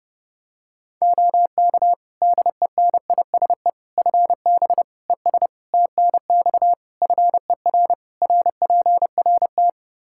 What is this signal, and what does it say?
Morse code, heard as a keyed tone, about 32 wpm, 700 Hz. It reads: OK DENISE FB ES TNX FER RPRT
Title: K1USN SST QSO Practice 30wpm